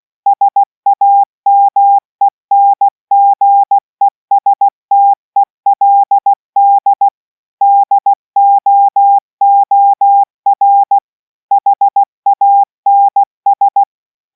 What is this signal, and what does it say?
Morse code, heard as a keyed tone, about 16 wpm, 800 Hz. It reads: SAMENGESTELD DOOR HANS